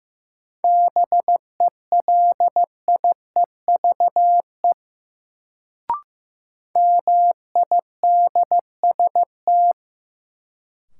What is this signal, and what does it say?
Morse code, heard as a keyed tone, about 15 wpm, 700 Hz. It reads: BELIEVE MIDST
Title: Sets of 4 Words - New Word Drill - 15wpm